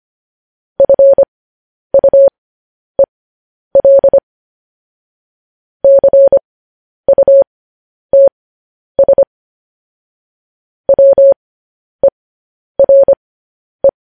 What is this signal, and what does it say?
Morse code, heard as a keyed tone, about 25 wpm, spaced as for 10 wpm, 550 Hz. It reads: FUEL CUTS WERE
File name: podcast / News headlines-1x-25x10